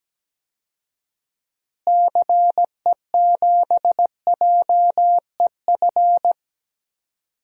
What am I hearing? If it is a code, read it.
Morse code, heard as a keyed tone, about 17 wpm, 700 Hz. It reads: CE7JEF
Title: DXCC-100 Call Signs - 6 Characters - Repeated 2x - 17wpm